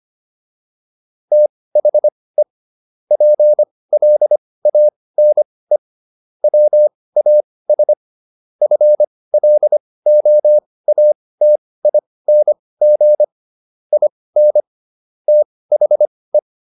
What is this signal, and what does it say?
Morse code, heard as a keyed tone, about 25 wpm, spaced as for 18 wpm, 600 Hz. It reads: THE PLANE WAS FLOATING IN THE